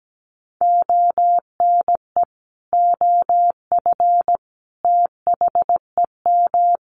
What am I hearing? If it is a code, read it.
Morse code, heard as a keyed tone, about 17 wpm, 700 Hz. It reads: ONE OF THEM
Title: Sentences from Top 1300 Words 17wpm